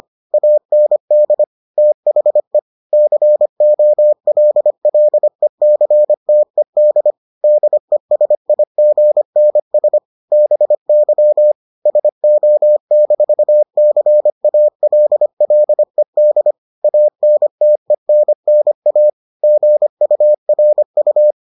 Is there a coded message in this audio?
Yes: Morse code from 0.3 to 21.3 s, heard as a keyed tone, about 25 wpm, 600 Hz. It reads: AND THE COLLECTED DESIGNS BY SO-CALLED ANTENNA GURU